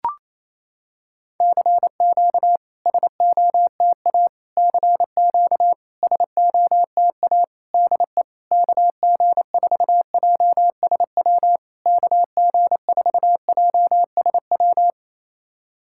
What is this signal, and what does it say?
Morse code, heard as a keyed tone, about 28 wpm, 700 Hz. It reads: CQ SOTA CQ SOTA DE KG4JSW KG4JSW
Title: SOTA QSO Practice 28wpm